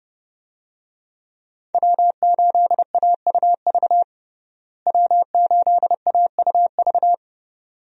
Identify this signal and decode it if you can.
Morse code, heard as a keyed tone, about 32 wpm, 700 Hz. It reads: W8AUV W8AUV
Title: USA Call Signs - 5 Characters - Repeated 2x - 30wpm